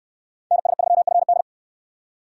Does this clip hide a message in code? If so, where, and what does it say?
Yes: Morse code from 0.5 to 1.4 s, heard as a keyed tone, about 69 wpm, 700 Hz. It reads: NS3FD